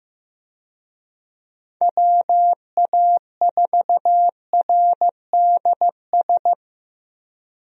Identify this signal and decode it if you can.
Morse code, heard as a keyed tone, about 15 wpm, 700 Hz. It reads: WA4RDS
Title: USA Call Signs - 6 Characters - 15wpm